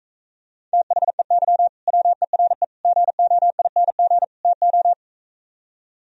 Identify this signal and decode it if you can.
Morse code, heard as a keyed tone, about 44 wpm, 700 Hz. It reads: THEY WERE GOING TO